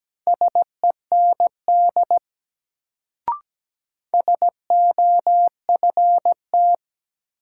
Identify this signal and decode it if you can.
Morse code, heard as a keyed tone, about 17 wpm, 700 Hz. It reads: SEND SOFT
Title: Sentences from Top 1500 Words - New Word Drill - 17wpm